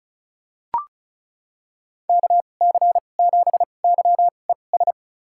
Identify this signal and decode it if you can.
Morse code, heard as a keyed tone, about 36 wpm, spaced as for 25 wpm, 700 Hz. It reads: KC7YES
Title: USA Call Signs - 6 Characters - 2x Character Spacing - 35wpm